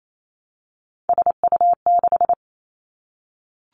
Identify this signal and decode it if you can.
Morse code, heard as a keyed tone, about 28 wpm, 700 Hz. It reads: SU6
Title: Three Challenging Letter-Number Combinations 28wpm